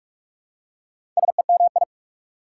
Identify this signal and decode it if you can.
Morse code, heard as a keyed tone, about 46 wpm, 690 Hz. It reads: SEMI
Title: Prefixes and Suffixes 45wpm